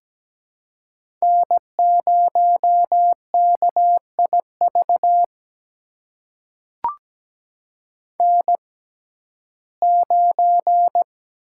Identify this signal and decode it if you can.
Morse code, heard as a keyed tone, about 17 wpm, spaced as for 12 wpm, 700 Hz. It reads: N0KIV N9
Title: USA Call Signs - 5 Characters - 6x Character Spacing - 17wpm